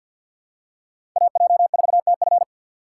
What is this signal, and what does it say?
Morse code, heard as a keyed tone, about 51 wpm, 700 Hz. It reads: AJ4TF
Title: K1USN SST QSO Practice - 50wpm